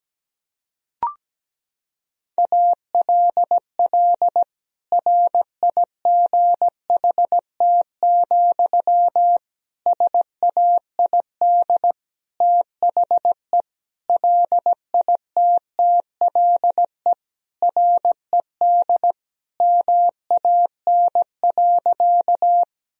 Morse code, heard as a keyed tone, about 17 wpm, 700 Hz. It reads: ALL RIGHT, SAID THE LITTLE RED MAN.
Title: Sentences from Top 1000 Words - Review Part 2 - 17wpm